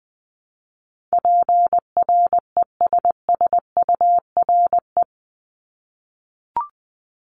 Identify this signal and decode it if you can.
Morse code, heard as a keyed tone, about 20 wpm, 700 Hz. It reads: PRESSURE
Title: Multi-letter Phonemes Repeated 3x 20wpm